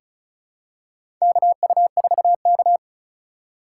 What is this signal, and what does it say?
Morse code, heard as a keyed tone, about 34 wpm, 700 Hz. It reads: KU4K
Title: USA Call Signs - 4 Characters - 2x Character Spacing - 35wpm